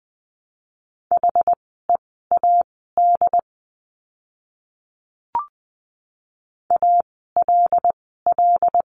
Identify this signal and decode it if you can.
Morse code, heard as a keyed tone, about 20 wpm, spaced as for 15 wpm, 700 Hz. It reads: HEAD ALL
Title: 075 Lesson 13f - Letter L - Words 20wpm